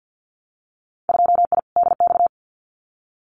Morse code, heard as a keyed tone, about 50 wpm, 700 Hz. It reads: 2I DX
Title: Winter Field Day Exchanges 50wpm